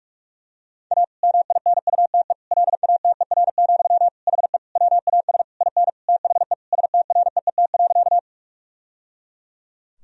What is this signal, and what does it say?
Morse code, heard as a keyed tone, about 41 wpm, 700 Hz. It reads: A MINUTE LATER, HE WAS IN THE STREET.
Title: Sentences from Top 1000 Words - Review Part 4 - 45wpm